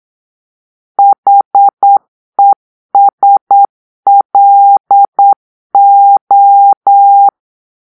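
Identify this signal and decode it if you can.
Morse code, heard as a keyed tone, about 9 wpm, 800 Hz. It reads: HESLO
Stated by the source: Morse-code cipher made by a player team for the Kachna puzzle game.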